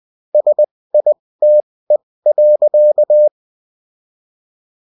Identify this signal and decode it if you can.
Morse code, heard as a keyed tone, about 20 wpm, spaced as for 16 wpm, 600 Hz. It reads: SITE.